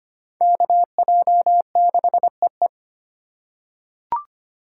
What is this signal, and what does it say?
Morse code, heard as a keyed tone, about 25 wpm, 700 Hz. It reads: KJ6EE
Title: USA Call Signs - 5 Characters - 3x Character Spacing - 25wpm